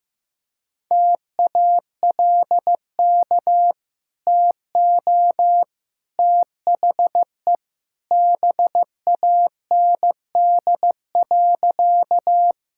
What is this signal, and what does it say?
Morse code, heard as a keyed tone, about 15 wpm, 700 Hz. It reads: TALK TO THE BAND.
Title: Sentences from Top 1600 Words 15wpm